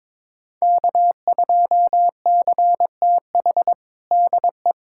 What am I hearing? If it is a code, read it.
Morse code, heard as a keyed tone, about 22 wpm, 700 Hz. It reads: K2CTH DE
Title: USA Callsign DE Callsign - 5 Characters - 22wpm